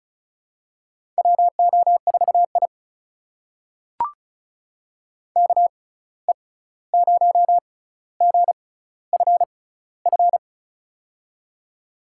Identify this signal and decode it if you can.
Morse code, heard as a keyed tone, about 36 wpm, spaced as for 16 wpm, 700 Hz. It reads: WO4I KE0GFF
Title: USA Call Signs - 6x Character Spacing - 35wpm